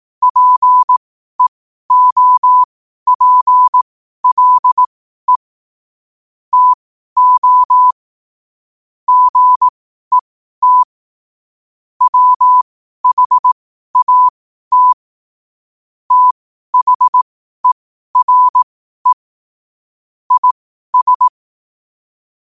Morse code, heard as a keyed tone, about 18 wpm, spaced as for 12 wpm, 1000 Hz. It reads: PEOPLE TO GET WHAT THERE IS
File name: qotdfiles / 12wpm